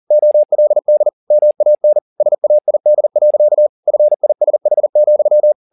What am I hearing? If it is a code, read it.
Morse code, heard as a keyed tone, about 40 wpm, 600 Hz. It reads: OLD MAN SAID. FISH,
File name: the old man and the sea 2 40wpm